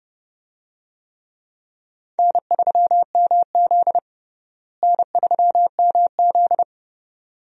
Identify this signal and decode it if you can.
Morse code, heard as a keyed tone, about 30 wpm, 700 Hz. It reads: N3MZ N3MZ
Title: USA Call Signs - 4 Characters - Repeated 2x - 30wpm